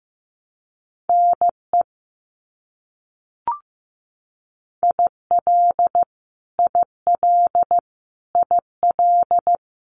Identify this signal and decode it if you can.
Morse code, heard as a keyed tone, about 15 wpm, 700 Hz. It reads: NE IL IL IL
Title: US States Repeated 3x 15wpm